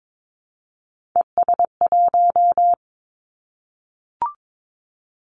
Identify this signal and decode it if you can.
Morse code, heard as a keyed tone, about 22 wpm, 700 Hz. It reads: ES1
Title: Callsign Prefixes for Top 50 DXCC - 22wpm